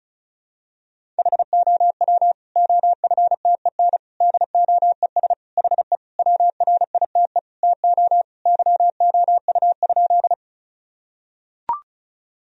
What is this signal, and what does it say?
Morse code, heard as a keyed tone, about 36 wpm, 700 Hz. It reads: HOW OFTEN DOES HE WRITE TO YOU?
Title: Sentences from Top 1000 Words - Review Part 1 - 35wpm